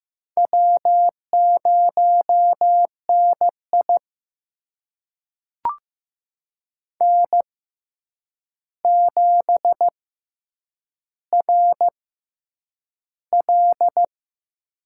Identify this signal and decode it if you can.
Morse code, heard as a keyed tone, about 15 wpm, spaced as for 8 wpm, 700 Hz. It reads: W0NI N7RL
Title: USA Call Signs - 4 Characters - 6x Character Spacing - 15wpm